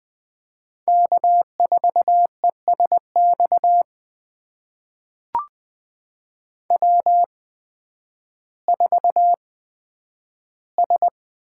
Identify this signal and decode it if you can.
Morse code, heard as a keyed tone, about 20 wpm, spaced as for 11 wpm, 700 Hz. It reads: K4ESX W4S
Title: USA Call Signs - 5 Characters - 8x Character Spacing - 20wpm